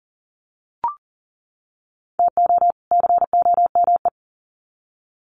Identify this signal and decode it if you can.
Morse code, heard as a keyed tone, about 40 wpm, 700 Hz. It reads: TO COME
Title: Sets of 2 Words 40wpm